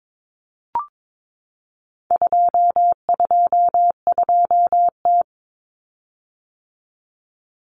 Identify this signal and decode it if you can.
Morse code, heard as a keyed tone, about 22 wpm, 700 Hz. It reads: 222T
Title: Common Cut Numbers - 4-Digit 22wpm